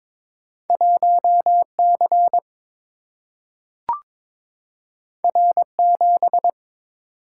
Morse code, heard as a keyed tone, about 22 wpm, 700 Hz. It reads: 1 C R 7
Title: Single-Letter Single-Number 22wpm